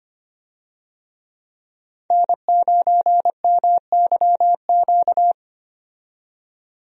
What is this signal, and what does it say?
Morse code, heard as a keyed tone, about 25 wpm, 700 Hz. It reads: N9MYQ
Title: USA Call Signs - 5 Characters - 25wpm